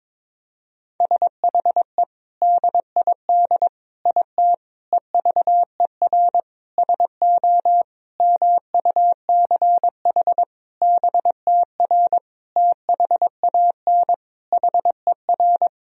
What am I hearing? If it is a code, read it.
Morse code, heard as a keyed tone, about 22 wpm, 700 Hz. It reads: SHE DID IT EVER SO MUCH BTR THAN HER
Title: Sentences from Top 400 Words and Abbreviations 22wpm